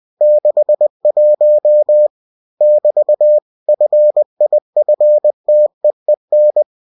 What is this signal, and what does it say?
Morse code, heard as a keyed tone, about 20 wpm, 600 Hz. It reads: 61 <BT> FIFTEEN
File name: cw-books / winnie-the-pooh / winnie-20wpm-page-0061